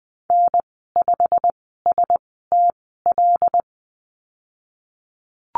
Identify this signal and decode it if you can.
Morse code, heard as a keyed tone, about 20 wpm, spaced as for 15 wpm, 700 Hz. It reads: N5STL
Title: 128 Lesson 21g - Number 6 - Call Signs 20wpm